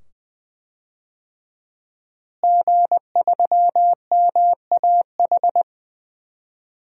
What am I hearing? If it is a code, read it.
Morse code, heard as a keyed tone, about 20 wpm, 700 Hz. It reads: G3MAH
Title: DXCC-100 Call Signs - 5 Characters - Repeated 2x - 20wpm